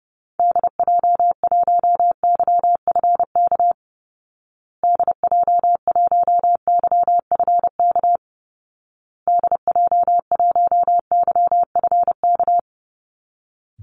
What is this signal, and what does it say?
Morse code, heard as a keyed tone, about 30 wpm, 700 Hz. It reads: DJ1YFK DJ1YFK DJ1YFK
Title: Call Signs Repeated 2x - 30wpm